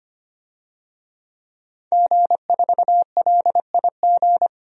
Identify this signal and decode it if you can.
Morse code, heard as a keyed tone, about 25 wpm, 700 Hz. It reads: G4LIG
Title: DXCC-100 Call Signs - 5 Characters - 25wpm